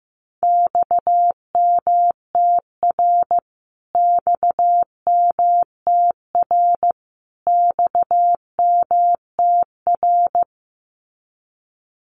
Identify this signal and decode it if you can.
Morse code, heard as a keyed tone, about 15 wpm, 700 Hz. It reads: XMTR XMTR XMTR
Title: Abbreviations Repeated 3x - 15wpm